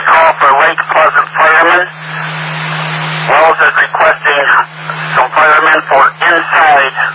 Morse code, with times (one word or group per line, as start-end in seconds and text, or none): none